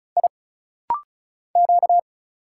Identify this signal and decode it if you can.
Morse code, heard as a keyed tone, about 36 wpm, 700 Hz. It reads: I Q
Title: Single Letters - Rapid-Fire 35wpm